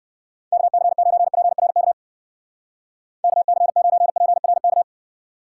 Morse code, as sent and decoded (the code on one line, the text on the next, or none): -.- -.-. ----. .--. .-. -.- / -.- -.-. ----. .--. .-. -.-
KC9PRK KC9PRK